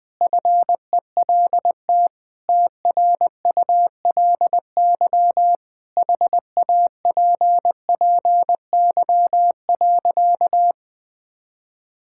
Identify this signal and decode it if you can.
Morse code, heard as a keyed tone, about 21 wpm, 700 Hz. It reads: FELT TRULY HAPPY.
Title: Sentences with 7-Word Phrases - 20wpm